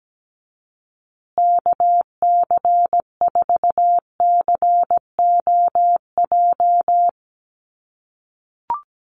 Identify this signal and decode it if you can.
Morse code, heard as a keyed tone, about 17 wpm, 700 Hz. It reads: KC4COJ
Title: USA Call Signs Repeated 2x 17wpm